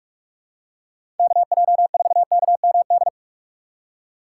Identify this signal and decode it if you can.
Morse code, heard as a keyed tone, about 47 wpm, 700 Hz. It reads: KJ4KMD
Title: USA Call Signs - 6 Characters - 4x Character Spacing - 45wpm